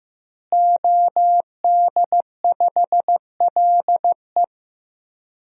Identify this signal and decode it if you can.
Morse code, heard as a keyed tone, about 15 wpm, 700 Hz. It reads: OD5LE
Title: DXCC-100 Call Signs - 5 Characters - Repeated 2x - 15wpm